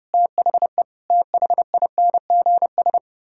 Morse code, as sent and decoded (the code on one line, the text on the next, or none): - .... / - .... .. -. --. ...
THE THINGS